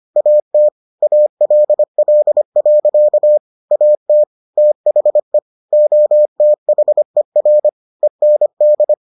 Morse code, heard as a keyed tone, about 25 wpm, 600 Hz. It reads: AT ALL. AT THE OTHER END